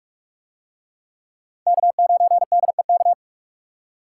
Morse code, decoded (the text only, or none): K9DEK